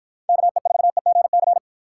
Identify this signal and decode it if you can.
Morse code, heard as a keyed tone, about 52 wpm, 700 Hz. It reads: KE4EGC